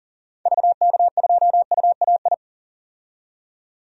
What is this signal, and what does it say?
Morse code, heard as a keyed tone, about 39 wpm, 700 Hz. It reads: VK2UAI
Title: DXCC-100 Call Signs - 6 Characters - 40wpm